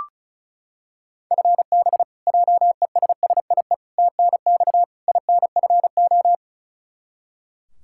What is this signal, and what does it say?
Morse code, heard as a keyed tone, about 36 wpm, 700 Hz. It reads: FB JESSIE TNX INFO
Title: Practice QSO - Exchange 3 - More Info - 35wpm